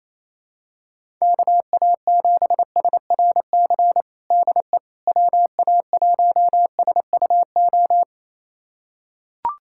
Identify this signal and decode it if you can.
Morse code, heard as a keyed tone, about 28 wpm, 700 Hz. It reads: KA7SRC DE WA1SUO
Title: USA Callsign DE Callsign - 6 Characters - 28wpm